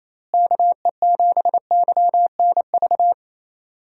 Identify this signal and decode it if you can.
Morse code, heard as a keyed tone, about 28 wpm, 700 Hz. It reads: KE7YNV
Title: USA Call Signs - 6 Characters - Repeated 3x - 28wpm